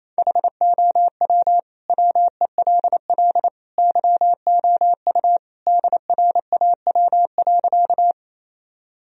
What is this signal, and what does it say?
Morse code, heard as a keyed tone, about 28 wpm, 700 Hz. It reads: HOW WELL YOU DRAW.
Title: Sentences from Top 1000 Words - Review Part 1 - 28wpm